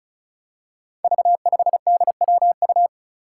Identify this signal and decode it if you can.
Morse code, heard as a keyed tone, about 36 wpm, 700 Hz. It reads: V5DWU
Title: Five Challenging Letter-Number Combinations 35wpm